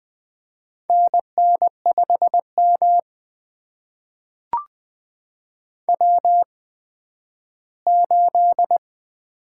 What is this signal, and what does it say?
Morse code, heard as a keyed tone, about 20 wpm, spaced as for 12 wpm, 700 Hz. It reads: NN5M W8
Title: USA Call Signs - 4 Characters - 8x Character Spacing - 20wpm